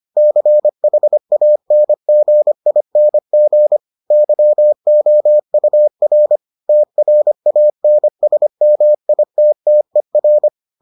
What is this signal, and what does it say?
Morse code, heard as a keyed tone, about 25 wpm, 600 Hz. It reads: CHANGING YOUR TRANSMITTER